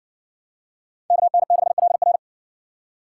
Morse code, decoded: KN6LR

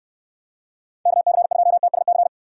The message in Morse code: KC2TSC

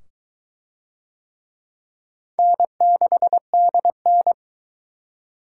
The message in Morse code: N6DN

